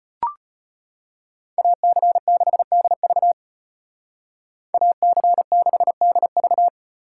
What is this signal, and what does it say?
Morse code, heard as a keyed tone, about 31 wpm, 700 Hz. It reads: AC6DV AC6DV
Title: Speed Racing - USA 2x2 Call Signs 25wpm